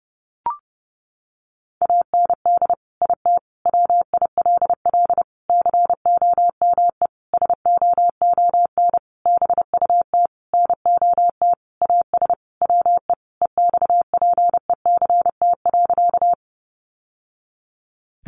Morse code, heard as a keyed tone, about 30 wpm, 700 Hz. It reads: AND IT WILL COME SOON BUT NOT AS WE EXPECT.
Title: Sentences from Top 1100 Words 30wpm